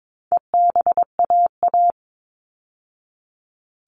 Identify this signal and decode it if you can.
Morse code, heard as a keyed tone, about 22 wpm, 700 Hz. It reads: EBAA
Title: All Cut Numbers - 4-Digit 22wpm